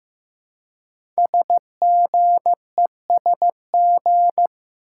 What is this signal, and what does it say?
Morse code, heard as a keyed tone, about 15 wpm, 700 Hz. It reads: SGESG
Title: Code Groups - 5 Letters - 15wpm